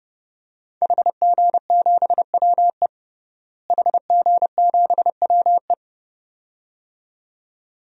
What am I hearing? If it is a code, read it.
Morse code, heard as a keyed tone, about 30 wpm, 700 Hz. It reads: HG7WE HG7WE
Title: DXCC-100 Call Signs - 5 Characters - Repeated 2x - 30wpm